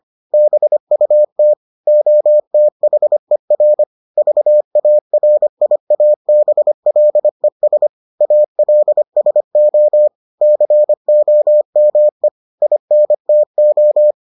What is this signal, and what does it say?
Morse code, heard as a keyed tone, about 25 wpm, 600 Hz. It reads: BUT OTHER VARIABLES ALSO COME INTO